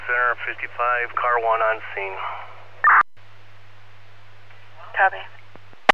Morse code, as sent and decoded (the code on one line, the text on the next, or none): none